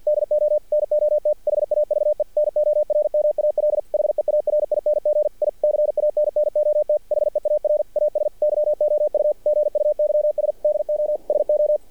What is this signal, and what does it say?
Morse code, heard as a keyed tone, about 50 wpm, 600 Hz. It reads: DO NOT HAVE NORMAL HEARING I CANNOT HEAR AS YOU GUYS DO SO
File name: archived / CRSnet